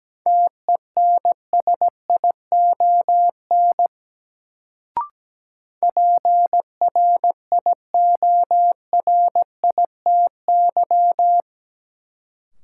Morse code, heard as a keyed tone, about 17 wpm, 700 Hz. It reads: TENSION PRIORITY
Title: Sentences from Top 1600 Words - New Word Drill - 17wpm